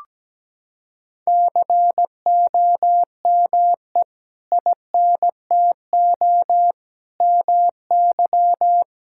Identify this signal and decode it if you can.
Morse code, heard as a keyed tone, about 17 wpm, 700 Hz. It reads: COME INTO MY
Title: Sentences from Top 1000 Words - Review Part 6 - 17wpm